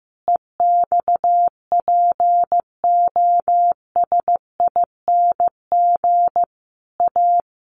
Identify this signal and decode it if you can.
Morse code, heard as a keyed tone, about 15 wpm, 700 Hz. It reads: EXPOSING A
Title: A Sound Of Thunder 15wpm